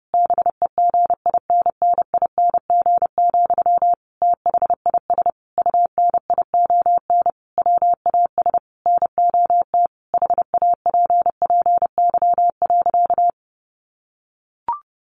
Morse code, heard as a keyed tone, about 30 wpm, 700 Hz. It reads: BEGINNING, THIS UNION WAS NOT HAPPY.
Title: Sentences from Top 1400 Words 30wpm